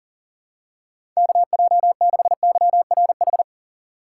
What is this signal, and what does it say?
Morse code, heard as a keyed tone, about 38 wpm, 700 Hz. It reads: KJ6YRH
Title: USA Call Signs - 6 Characters - 3x Character Spacing - 40wpm